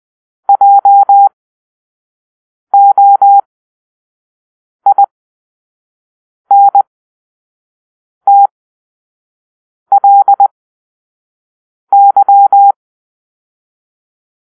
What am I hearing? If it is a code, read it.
Morse code, heard as a keyed tone, about 20 wpm, spaced as for 5 wpm, 800 Hz. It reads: JOINTLY